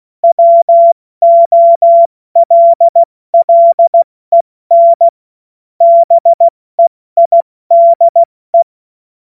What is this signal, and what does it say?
Morse code, heard as a keyed tone, about 16 wpm, 680 Hz. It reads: WOLLEN BEIDE
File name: Morsekurs / Witze / Witz 17. Januar 2023 70 F80